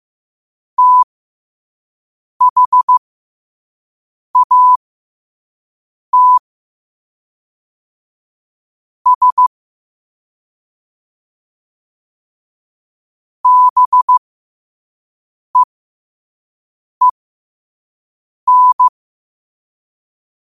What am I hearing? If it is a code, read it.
Morse code, heard as a keyed tone, about 15 wpm, spaced as for 5 wpm, 1000 Hz. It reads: THAT S BEEN
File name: qotdfiles / 5wpm